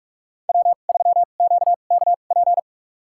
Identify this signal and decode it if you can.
Morse code, heard as a keyed tone, about 46 wpm, spaced as for 32 wpm, 700 Hz. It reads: W3QKP
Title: USA Call Signs - 5 Characters - 2x Character Spacing - 45wpm